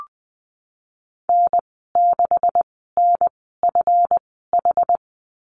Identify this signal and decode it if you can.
Morse code, heard as a keyed tone, about 20 wpm, spaced as for 15 wpm, 700 Hz. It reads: N6NFH
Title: 186 Lesson 30g - Letter b - Call Signs 20wpm